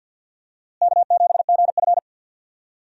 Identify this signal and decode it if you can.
Morse code, heard as a keyed tone, about 52 wpm, 700 Hz. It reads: K7GF
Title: USA Call Signs - 4 Characters - Repeated 3x - 50wpm